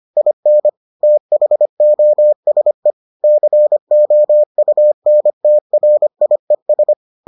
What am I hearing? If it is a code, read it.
Morse code, heard as a keyed tone, about 25 wpm, 600 Hz. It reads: IN THOSE COUNTRIES